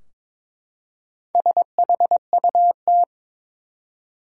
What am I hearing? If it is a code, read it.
Morse code, heard as a keyed tone, about 22 wpm, 700 Hz. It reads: SHUT
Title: Four Letter Words - 22wpm